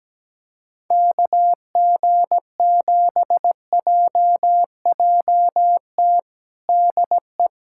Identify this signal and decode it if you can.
Morse code, heard as a keyed tone, about 17 wpm, 700 Hz. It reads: KG7JJT DE
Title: USA Callsign DE Callsign - 6 Characters - 17wpm